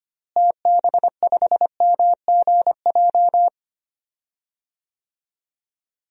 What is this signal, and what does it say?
Morse code, heard as a keyed tone, about 25 wpm, 700 Hz. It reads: TB5MGJ
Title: DXCC-100 Call Signs - 6 Characters - Repeated 3x - 25wpm